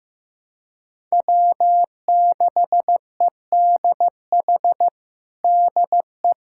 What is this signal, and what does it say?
Morse code, heard as a keyed tone, about 15 wpm, 700 Hz. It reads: W6EDH DE
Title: USA Callsign DE Callsign 15wpm